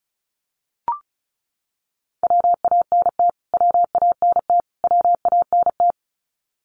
Morse code, heard as a keyed tone, about 34 wpm, 700 Hz. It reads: WANT WANT WANT
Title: Top 200 Words Repeated 35wpm